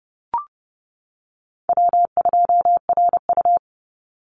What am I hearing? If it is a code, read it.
Morse code, heard as a keyed tone, about 30 wpm, 700 Hz. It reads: W2RU
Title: Easy Call Signs - 30wpm